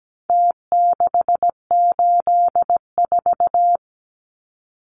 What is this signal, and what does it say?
Morse code, heard as a keyed tone, about 17 wpm, 700 Hz. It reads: T684
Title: Numbers - Contest Style - 4-Digit 17wpm